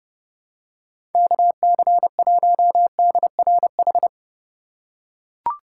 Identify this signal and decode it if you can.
Morse code, heard as a keyed tone, about 29 wpm, 700 Hz. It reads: KC1DRH
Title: USA Call Signs - 6 Characters - 2x Character Spacing - 30wpm